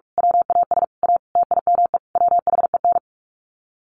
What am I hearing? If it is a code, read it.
Morse code, heard as a keyed tone, about 44 wpm, 700 Hz. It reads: WAS A TIME WHEN
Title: Sets of 5 Words 45wpm